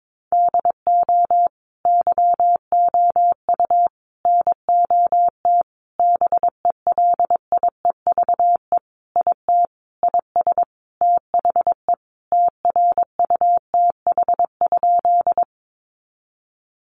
Morse code, heard as a keyed tone, about 22 wpm, 700 Hz. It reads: DO YOU NOT BELIEVE IT IS THE TRUTH?
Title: Sentences from Top 1000 Words 22wpm